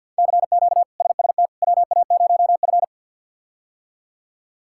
Morse code, heard as a keyed tone, about 47 wpm, 700 Hz. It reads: CQ SST WA0F
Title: K1USN SST QSO Practice - 50wpm